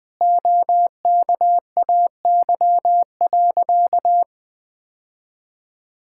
Morse code, heard as a keyed tone, about 20 wpm, 700 Hz. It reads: OKAY.
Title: Sentences with 2-Word Phrases - Spoken Beginning - 20wpm